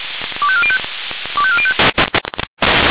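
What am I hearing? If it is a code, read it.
Five-tone selective call (ZVEI), heard as quick runs of five tones, tones of about 70 ms. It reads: 25405 25405
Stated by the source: Five-tone ZVEI selective call received on an SDR.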